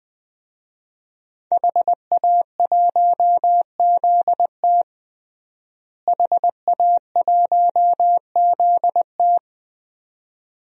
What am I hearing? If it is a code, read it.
Morse code, heard as a keyed tone, about 20 wpm, 700 Hz. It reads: HA1ZT HA1ZT
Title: DXCC-100 Call Signs - 5 Characters - Repeated 2x - 20wpm